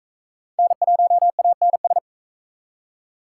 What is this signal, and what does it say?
Morse code, heard as a keyed tone, about 38 wpm, 700 Hz. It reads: N1ANS